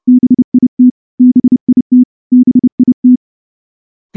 Morse code, heard as a keyed tone, about 28 wpm, 250 Hz. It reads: BIT BIT BIT
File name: morse-training-30WPM